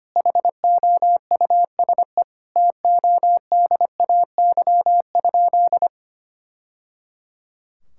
Morse code, heard as a keyed tone, about 25 wpm, 700 Hz. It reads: HOUSE TODAY?
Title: Sentences from Top 1000 Words - Review Part 6 - 25wpm